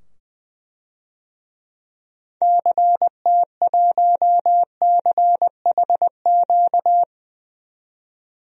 Morse code, heard as a keyed tone, about 20 wpm, 700 Hz. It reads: CT1CHQ
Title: DXCC-100 Call Signs - 6 Characters - 20wpm